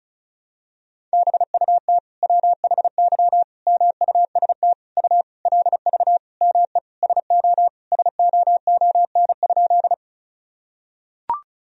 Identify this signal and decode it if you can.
Morse code, heard as a keyed tone, about 34 wpm, 700 Hz. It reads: BUT WHY MUST U LV ME SO SOON?
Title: Sentences from Top 400 Words and Abbreviations 35wpm